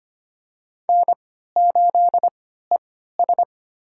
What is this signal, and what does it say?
Morse code, heard as a keyed tone, about 25 wpm, spaced as for 14 wpm, 700 Hz. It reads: N8ES
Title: USA Call Signs - 4 Characters - 3x Character Spacing - 25wpm